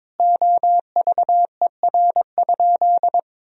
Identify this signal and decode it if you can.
Morse code, heard as a keyed tone, about 23 wpm, 700 Hz. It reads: OVER?
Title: 3x Word-Spacing - Sentences from Top 100 Words - 22wpm